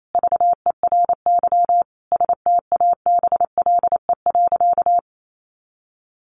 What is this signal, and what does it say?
Morse code, heard as a keyed tone, about 28 wpm, 700 Hz. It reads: VERY STABLE.
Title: Sentences from Top 1500 Words 28wpm